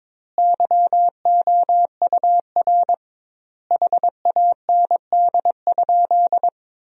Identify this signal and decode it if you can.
Morse code, heard as a keyed tone, about 22 wpm, 700 Hz. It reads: YOUR HAND?
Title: 2x Word-Spacing - Sentences from Top 200 Words - 22wpm